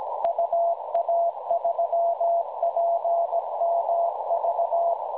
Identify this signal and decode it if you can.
Morse code, heard as a keyed tone, about 17 wpm, 680 Hz, with noise. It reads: UA3PMV
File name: attachments / UA3PMV websdr recording 7023.1kHz